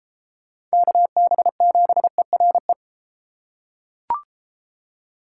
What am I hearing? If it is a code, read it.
Morse code, heard as a keyed tone, about 33 wpm, 700 Hz. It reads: KB7ERE